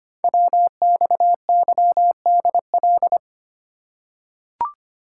Morse code, heard as a keyed tone, about 25 wpm, 700 Hz. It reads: WXYDL
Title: Code Groups - 5 Letters - 25wpm